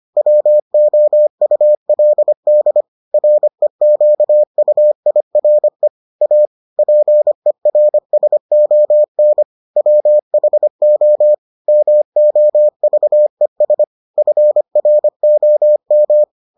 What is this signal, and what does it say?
Morse code, heard as a keyed tone, about 25 wpm, 600 Hz. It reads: WOULD REQUIRE A PERSON WHO MOVES FROM